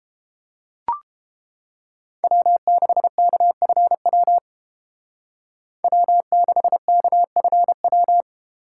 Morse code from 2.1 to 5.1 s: W6KFW

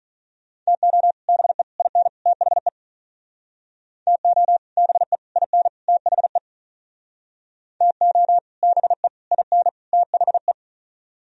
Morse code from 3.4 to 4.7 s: TO

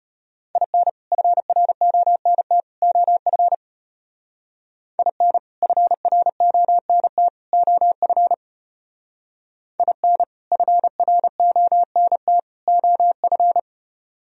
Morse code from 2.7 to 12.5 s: OF IN FRONT OF IN FRONT

